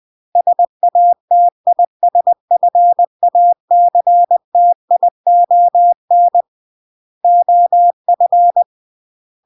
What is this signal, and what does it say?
Morse code, heard as a keyed tone, about 20 wpm, 700 Hz. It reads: SATISFACTION OF